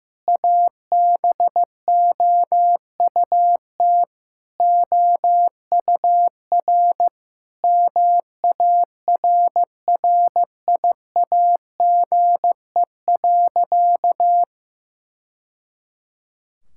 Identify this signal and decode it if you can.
Morse code, heard as a keyed tone, about 15 wpm, 700 Hz. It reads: ABOUT OUR MARRIAGE.
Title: Sentences from Top 1000 Words - Review Part 5 - 15wpm